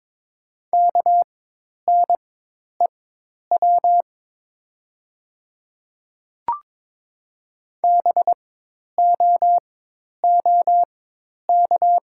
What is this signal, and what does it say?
Morse code, heard as a keyed tone, about 22 wpm, spaced as for 10 wpm, 700 Hz. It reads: KNEW BOOK
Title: 4x Character Spacing - Top 500 Words - 22wpm